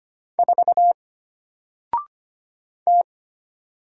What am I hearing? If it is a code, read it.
Morse code, heard as a keyed tone, about 25 wpm, 700 Hz. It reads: 4 T
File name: Single-Letter-Number-25wpm